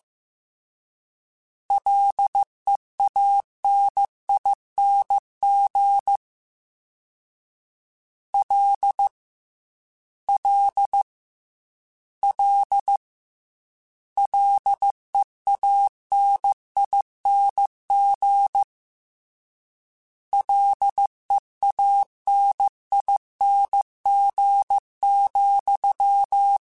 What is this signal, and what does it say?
Morse code, heard as a keyed tone, about 15 wpm, 800 Hz. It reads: LEANING L L L LEANING LEANING,
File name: Combinations / Combinations-mp3s / Combinations@13